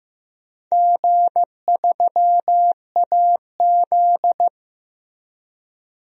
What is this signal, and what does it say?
Morse code, heard as a keyed tone, about 15 wpm, 700 Hz. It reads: G3AZ
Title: DXCC-100 Call Signs - 4 Characters - 15wpm